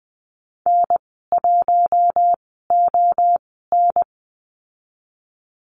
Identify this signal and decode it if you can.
Morse code, heard as a keyed tone, about 20 wpm, spaced as for 15 wpm, 700 Hz. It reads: N1ON
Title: 042 Lesson 8g - Number 1 - Call Signs 20wpm